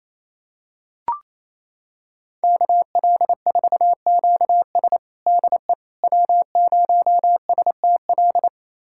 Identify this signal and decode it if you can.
Morse code, heard as a keyed tone, about 28 wpm, 700 Hz. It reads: KL4QS DE W0STL
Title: USA Callsign DE Callsign - 5 Characters - 28wpm